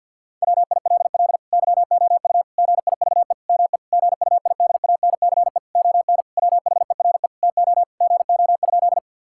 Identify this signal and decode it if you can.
Morse code, heard as a keyed tone, about 50 wpm, 700 Hz. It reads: WILL YOU GIVE ME GUIDANCE ON WHERE TO GO?
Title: Sentences from Top 1600 Words 50wpm